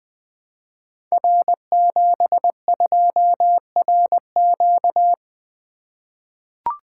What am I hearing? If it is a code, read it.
Morse code, heard as a keyed tone, about 20 wpm, 700 Hz. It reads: R72RQ
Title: Code Groups - 5 Letter:Numbers - 20wpm